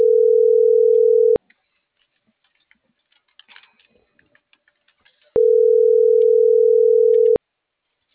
Ringback tone (North America), 440 + 480 Hz.